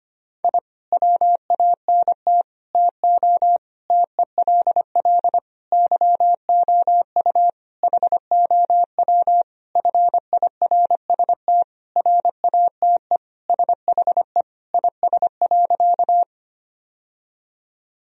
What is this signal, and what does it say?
Morse code, heard as a keyed tone, about 25 wpm, 700 Hz. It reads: I WANT TO TELL YOU HOW FIRST RATE SHE IS.
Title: Sentences from Top 1000 Words - Review Part 4 - 25wpm